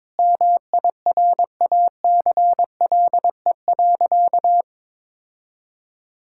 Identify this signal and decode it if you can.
Morse code, heard as a keyed tone, about 21 wpm, 700 Hz. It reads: MIRACLE.
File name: Sentences with 7-Word Phrases - 22wpm